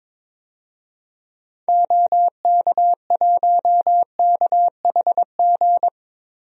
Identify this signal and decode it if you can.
Morse code, heard as a keyed tone, about 22 wpm, 700 Hz. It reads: OK1KHG
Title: DXCC-100 Call Signs - 6 Characters - 22wpm